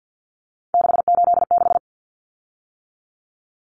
Morse code, heard as a keyed tone, about 50 wpm, 700 Hz. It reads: 686
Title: Common Cut Numbers - 3-Digit 50wpm